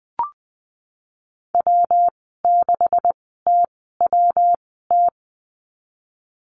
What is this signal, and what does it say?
Morse code, heard as a keyed tone, about 20 wpm, spaced as for 15 wpm, 700 Hz. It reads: W6TWT
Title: 186 Lesson 30g - Letter b - Call Signs 20wpm